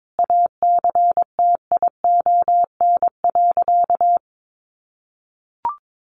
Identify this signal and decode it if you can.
Morse code, heard as a keyed tone, about 22 wpm, 700 Hz. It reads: ACTION.
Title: Sentences from Top 1300 Words 22wpm